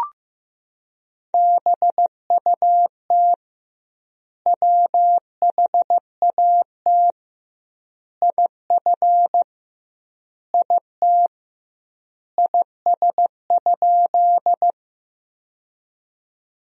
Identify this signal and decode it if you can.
Morse code, heard as a keyed tone, about 16 wpm, 700 Hz. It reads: BUT WHAT IF IT IS?
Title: 2x Word-Spacing - Sentences from Top 100 Words - 15wpm